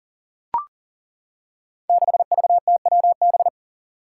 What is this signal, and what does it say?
Morse code, heard as a keyed tone, about 39 wpm, 700 Hz. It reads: 6VTWB
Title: Five Challenging Letter-Number Combinations 40wpm